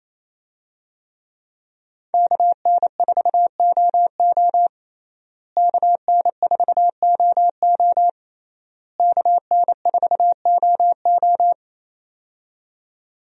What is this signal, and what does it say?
Morse code, heard as a keyed tone, about 28 wpm, 700 Hz. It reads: KN4OO KN4OO KN4OO
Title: USA Call Signs - 5 Characters - Repeated 3x - 28wpm